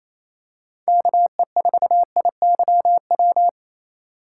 Morse code, heard as a keyed tone, about 28 wpm, 700 Hz. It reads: KE4IYW
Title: USA Call Signs - 6 Characters - 2x Character Spacing - 28wpm